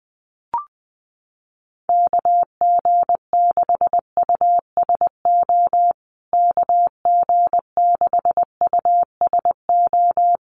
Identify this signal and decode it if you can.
Morse code, heard as a keyed tone, about 20 wpm, 700 Hz. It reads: KG6USO KG6USO
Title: USA Call Signs Repeated 3x 20wpm